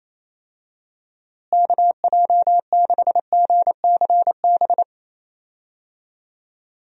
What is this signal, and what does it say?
Morse code, heard as a keyed tone, about 28 wpm, 700 Hz. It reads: KJ6GCB
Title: USA Call Signs - 6 Characters - Repeated 2x - 28wpm